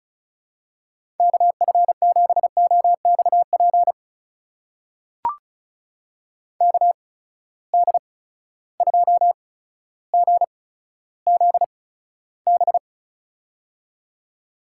Morse code, heard as a keyed tone, about 36 wpm, spaced as for 15 wpm, 700 Hz. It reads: KF7OXP KD2GZB
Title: USA Call Signs - 8x Character Spacing - 35wpm